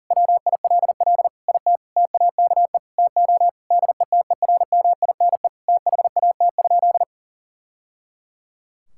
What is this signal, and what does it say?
Morse code, heard as a keyed tone, about 37 wpm, 700 Hz. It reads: WILL IT TAKE TO DETERMINE THAT?
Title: Sentences from Top 1000 Words - Review Part 5 - 40wpm